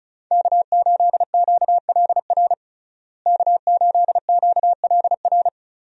Morse code, heard as a keyed tone, about 36 wpm, 700 Hz. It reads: K8QLR K8QLR
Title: USA Call Signs - 5 Characters - Repeated 3x - 35wpm